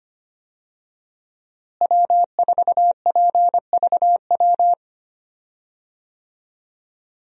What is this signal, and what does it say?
Morse code, heard as a keyed tone, about 25 wpm, 700 Hz. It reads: W4PVW
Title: USA Call Signs - 5 Characters - Repeated 3x - 25wpm